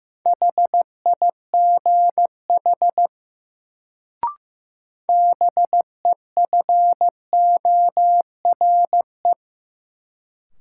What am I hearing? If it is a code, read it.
Morse code, heard as a keyed tone, about 15 wpm, 700 Hz. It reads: HIGH BEFORE
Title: Sentences from Top 200 Words - New Word Drill - 15wpm